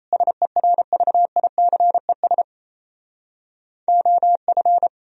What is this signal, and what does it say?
Morse code, heard as a keyed tone, about 38 wpm, 700 Hz. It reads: SERVICES OF